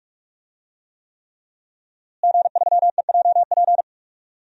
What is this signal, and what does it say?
Morse code, heard as a keyed tone, about 46 wpm, 700 Hz. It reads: G3EJP
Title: DXCC-100 Call Signs - 5 Characters - Repeated 3x - 45wpm